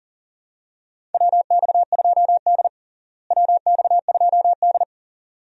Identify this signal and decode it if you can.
Morse code, heard as a keyed tone, about 40 wpm, 700 Hz. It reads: WX2D WX2D
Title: USA Call Signs - 4 Characters - Repeated 2x - 40wpm